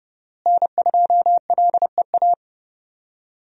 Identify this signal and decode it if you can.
Morse code, heard as a keyed tone, about 30 wpm, 700 Hz. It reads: N2LEA
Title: USA Call Signs - 5 Characters - Repeated 2x - 30wpm